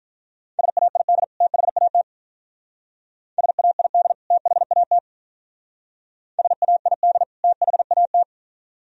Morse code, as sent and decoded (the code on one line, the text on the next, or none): ... .- .. -.. / - .... .- - / ... .- .. -.. / - .... .- - / ... .- .. -.. / - .... .- -
SAID THAT SAID THAT SAID THAT